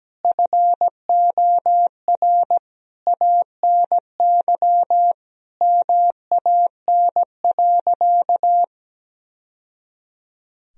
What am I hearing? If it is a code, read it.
Morse code, heard as a keyed tone, about 17 wpm, 700 Hz. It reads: FOR ANY MAN.
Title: Sentences from Top 1000 Words - Review Part 3 - 17wpm